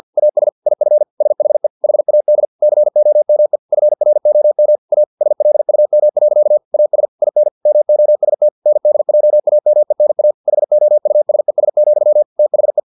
Morse code, heard as a keyed tone, about 50 wpm, 600 Hz. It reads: AS IF SHE HAD COME FROM A SLUM. AS IN MOST NDJAMENA HOUSES, THE
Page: Never 49wpm